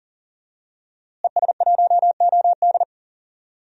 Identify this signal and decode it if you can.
Morse code, heard as a keyed tone, about 40 wpm, 700 Hz. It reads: ES1OD
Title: DXCC-100 Call Signs - 5 Characters - Repeated 2x - 40wpm